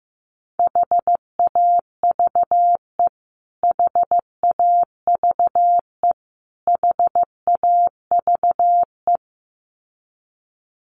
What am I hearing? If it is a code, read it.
Morse code, heard as a keyed tone, about 15 wpm, 700 Hz. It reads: HAVE HAVE HAVE
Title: Top 200 Words Repeated 15wpm